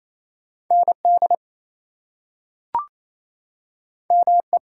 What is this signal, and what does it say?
Morse code, heard as a keyed tone, about 28 wpm, 700 Hz. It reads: ND ME
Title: US States and Canadian Provinces 28wpm